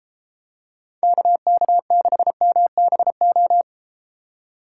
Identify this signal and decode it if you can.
Morse code, heard as a keyed tone, about 32 wpm, 700 Hz. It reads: KK6MBO